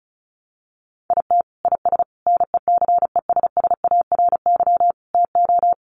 Morse code, heard as a keyed tone, about 34 wpm, 700 Hz. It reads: IT IS NECESSARY TO
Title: Sentences from Top 800 Words 35wpm